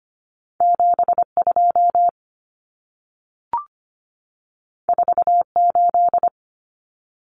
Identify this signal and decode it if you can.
Morse code, heard as a keyed tone, about 25 wpm, 700 Hz. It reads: 72 48
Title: Numbers - 2-Digit 25wpm